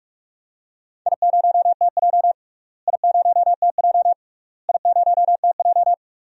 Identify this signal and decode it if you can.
Morse code, heard as a keyed tone, about 43 wpm, 700 Hz. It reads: I0TJ I0TJ I0TJ
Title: DXCC-100 Call Signs - 4 Characters - Repeated 3x - 45wpm